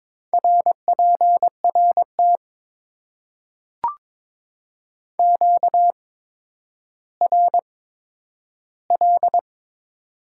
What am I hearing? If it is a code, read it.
Morse code, heard as a keyed tone, about 22 wpm, spaced as for 11 wpm, 700 Hz. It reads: RPRT QRL
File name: Top 90 QSO Elements - 8x Character Spacing - 22wpm